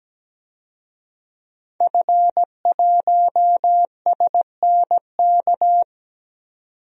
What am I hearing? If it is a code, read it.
Morse code, heard as a keyed tone, about 17 wpm, 700 Hz. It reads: F1SNK
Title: DXCC-100 Call Signs - 5 Characters - 17wpm